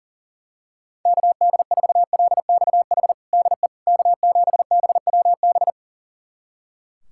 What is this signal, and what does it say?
Morse code, heard as a keyed tone, about 40 wpm, 700 Hz. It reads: KD4LXH DE K7BWB
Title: USA Callsign DE Callsign 40wpm